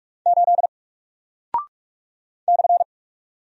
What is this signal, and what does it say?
Morse code, heard as a keyed tone, about 43 wpm, 700 Hz. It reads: 8 /
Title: Single Character (Letter:Number:Punctuation:Prosign) Alt 45wpm